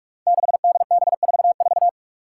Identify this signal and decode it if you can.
Morse code, heard as a keyed tone, about 46 wpm, 700 Hz. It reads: 6DB44
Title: Five Challenging Letter-Number Combinations 45wpm